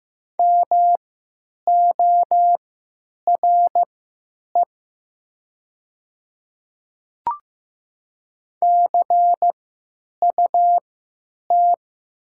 Morse code, heard as a keyed tone, about 15 wpm, spaced as for 9 wpm, 700 Hz. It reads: MORE CUT
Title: 3x Character Spacing - Top 300 Words - 15wpm